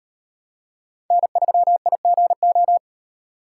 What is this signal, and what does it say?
Morse code, heard as a keyed tone, about 38 wpm, 700 Hz. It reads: N3IGO